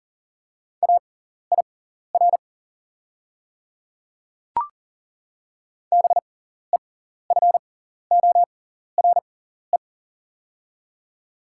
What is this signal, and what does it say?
Morse code, heard as a keyed tone, about 40 wpm, spaced as for 14 wpm, 700 Hz. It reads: AIR BEFORE
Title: 6x Character Spacing - Top 200 Words - 40wpm